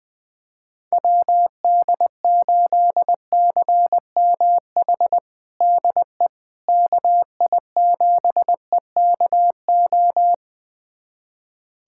Morse code, heard as a keyed tone, about 20 wpm, 700 Hz. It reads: WD8CMH DE KI7EKO
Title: USA Callsign DE Callsign 20wpm